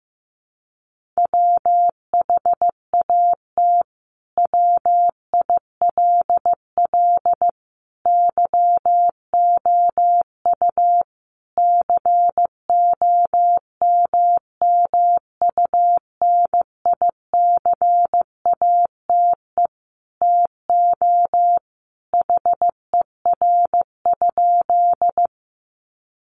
Morse code, heard as a keyed tone, about 15 wpm, 700 Hz. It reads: WHAT WILL YOU COMMUNICATE TO HER?
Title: Sentences from Top 1300 Words 15wpm